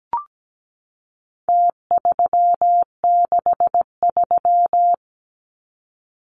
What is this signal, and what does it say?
Morse code, heard as a keyed tone, about 17 wpm, 700 Hz. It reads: T363
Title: Numbers - Contest Style - 4-Digit 17wpm